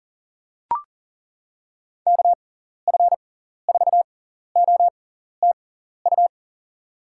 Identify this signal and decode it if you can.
Morse code, heard as a keyed tone, about 40 wpm, spaced as for 14 wpm, 700 Hz. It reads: KF4OTU
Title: USA Call Signs - 6 Characters - 6x Character Spacing - 40wpm